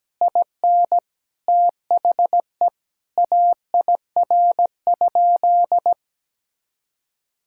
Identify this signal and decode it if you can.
Morse code, heard as a keyed tone, about 17 wpm, 700 Hz. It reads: IN THE AIR?
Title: Sentences from Top 1000 Words - Review Part 6 - 17wpm